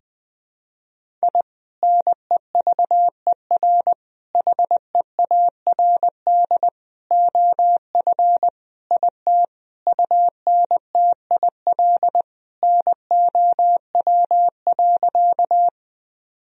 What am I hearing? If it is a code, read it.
Morse code, heard as a keyed tone, about 20 wpm, 700 Hz. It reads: I NEVER HEARD OF IT UNTIL NOW.
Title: Sentences from Top 1000 Words - Review Part 1 - 20wpm